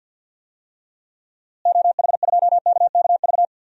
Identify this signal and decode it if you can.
Morse code, heard as a keyed tone, about 46 wpm, 700 Hz. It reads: OH2KKV
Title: DXCC-100 Call Signs - 6 Characters - Repeated 3x - 50wpm